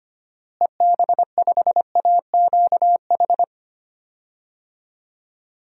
Morse code, heard as a keyed tone, about 25 wpm, 700 Hz. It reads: EB5AQH
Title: DXCC-100 Call Signs - 6 Characters - Repeated 3x - 25wpm